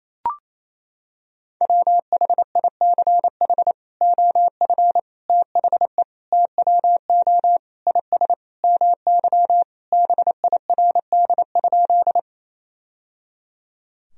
Morse code, heard as a keyed tone, about 27 wpm, 700 Hz. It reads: WHICH OF THE TWO IS MY BIRD?
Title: Sentences from Top 1000 Words - Review Part 2 - 28wpm